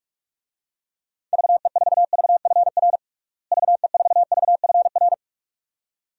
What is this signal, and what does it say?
Morse code, heard as a keyed tone, about 46 wpm, 700 Hz. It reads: VE4VFR VE4VFR
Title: DXCC-100 Call Signs - 6 Characters - Repeated 2x - 45wpm